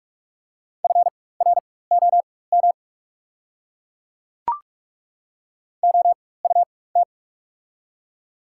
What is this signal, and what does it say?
Morse code, heard as a keyed tone, about 43 wpm, spaced as for 21 wpm, 690 Hz. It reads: FROM OUT
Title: 4x Character Spacing - Top 200 Words - 45wpm